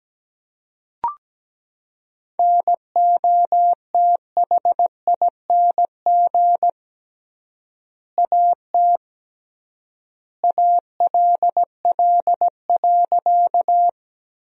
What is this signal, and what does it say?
Morse code, heard as a keyed tone, about 17 wpm, 700 Hz. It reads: NOTHING AT ALL.
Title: 3x Word-Spacing - Sentences from Top 500 Words - 17wpm